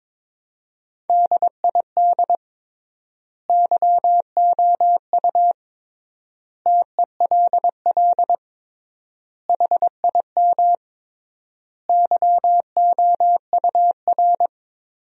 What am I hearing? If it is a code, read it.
Morse code, heard as a keyed tone, about 21 wpm, 700 Hz. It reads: DID YOU TELL HIM YOUR
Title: 3x Word-Spacing - Sentences from Top 400 Words - 22wpm